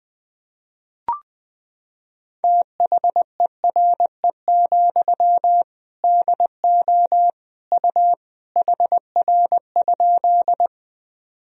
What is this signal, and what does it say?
Morse code, heard as a keyed tone, about 20 wpm, 700 Hz. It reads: THERE, DO U HR?
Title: Sentences from Top 300 Words and Abbreviations 20wpm